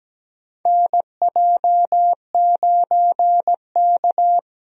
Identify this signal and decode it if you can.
Morse code, heard as a keyed tone, about 17 wpm, 700 Hz. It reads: NJ9K
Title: USA Call Signs - 4 Characters - Repeated 2x - 17wpm